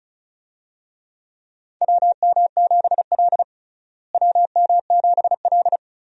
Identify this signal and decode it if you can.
Morse code, heard as a keyed tone, about 36 wpm, 700 Hz. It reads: WM7L WM7L
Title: USA Call Signs - 4 Characters - Repeated 3x - 35wpm